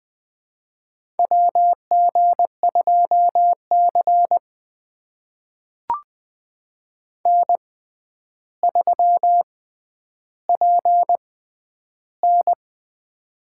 Morse code, heard as a keyed tone, about 20 wpm, spaced as for 10 wpm, 700 Hz. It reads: WG2C N3PN
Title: USA Call Signs - 4 Characters - 6x Character Spacing - 20wpm